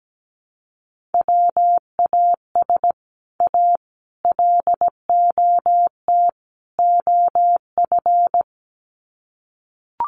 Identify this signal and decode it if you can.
Morse code, heard as a keyed tone, about 17 wpm, 700 Hz. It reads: WAS A LOT OF
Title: Sets of 4 Words 17wpm